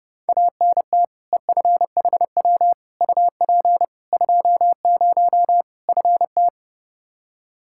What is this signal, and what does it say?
Morse code, heard as a keyed tone, about 30 wpm, 700 Hz. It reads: ANT EFHW UP 20 FT
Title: Practice QSO - Exchange 2 - Info - 30wpm